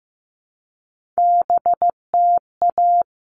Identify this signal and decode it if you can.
Morse code, heard as a keyed tone, about 15 wpm, 700 Hz. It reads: BTA
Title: All Cut Numbers - 3-Digit 15wpm